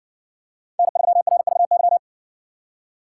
Morse code, heard as a keyed tone, about 60 wpm, 700 Hz. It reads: N3RVY